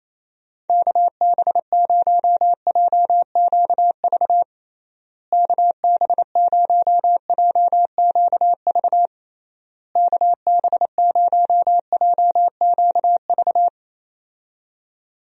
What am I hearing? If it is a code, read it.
Morse code, heard as a keyed tone, about 28 wpm, 700 Hz. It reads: KB0JQV KB0JQV KB0JQV
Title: USA Call Signs - 6 Characters - Repeated 3x - 28wpm